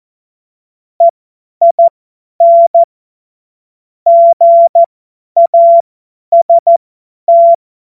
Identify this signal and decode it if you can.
Morse code, heard as a keyed tone, about 14 wpm, spaced as for 10 wpm, 680 Hz. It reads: EIN GAST
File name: Morsekurs / Witze / Witz 25. Mai 2021, 50 F70